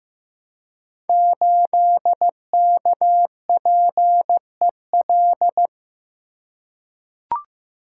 Morse code, heard as a keyed tone, about 15 wpm, 700 Hz. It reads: 8KPEL
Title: Code Groups - 5 Letter:Numbers - 15wpm